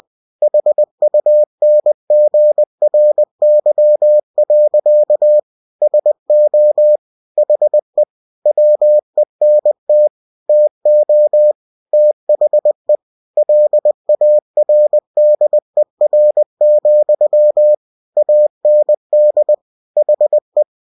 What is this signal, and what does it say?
Morse code, heard as a keyed tone, about 20 wpm, 600 Hz. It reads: HUNGRY. SO HE WENT TO THE LARDER, AND HE
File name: cw-books / winnie-the-pooh / winnie-20wpm-page-0039